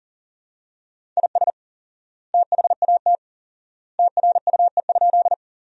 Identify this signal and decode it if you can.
Morse code, heard as a keyed tone, about 40 wpm, 700 Hz. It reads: IS THAT TRUE?
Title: 4x Word-Spacing - Sentences from Top 400 Words - 40wpm